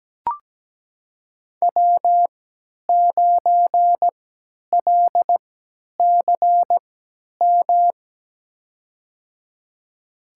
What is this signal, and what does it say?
Morse code, heard as a keyed tone, about 17 wpm, spaced as for 10 wpm, 700 Hz. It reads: W9LCM
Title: USA Call Signs - 5 Characters - 3x Character Spacing - 17wpm